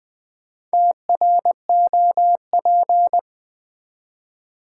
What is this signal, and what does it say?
Morse code, heard as a keyed tone, about 20 wpm, 700 Hz. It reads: TROP
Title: Build A Word - 2 Letters At A Time - 6-Letter Words - 20wpm